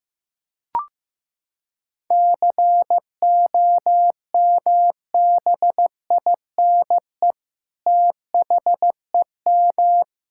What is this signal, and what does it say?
Morse code, heard as a keyed tone, about 15 wpm, 700 Hz. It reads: COMBINE THEM
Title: Sentences from Top 1600 Words 15wpm